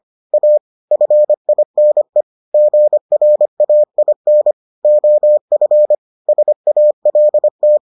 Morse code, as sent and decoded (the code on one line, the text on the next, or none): .- / ..-. .. -. / --. .-. .- .. -. / --- ..-. / ... .- .-.. -
A FINE GRAIN OF SALT